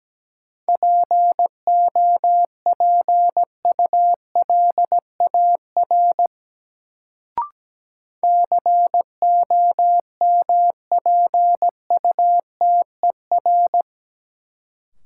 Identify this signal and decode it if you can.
Morse code, heard as a keyed tone, about 17 wpm, 700 Hz. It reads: POPULAR COMPUTER
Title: Sentences from Top 1000 Words - Review Part 3 - New Word Drill - 17wpm